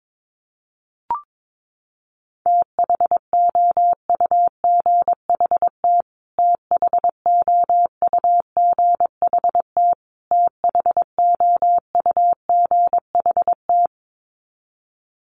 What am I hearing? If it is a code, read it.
Morse code, heard as a keyed tone, about 22 wpm, 700 Hz. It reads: THOUGHT THOUGHT THOUGHT
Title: Top 500 Words Repeated 22wpm